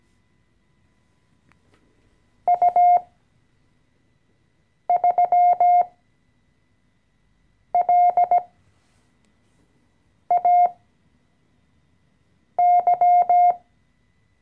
Morse code, heard as a keyed tone, about 17 wpm, spaced as for 4 wpm, 710 Hz. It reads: U3LAY
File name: CW-Online-Kurs / Anfänger / Lektion 014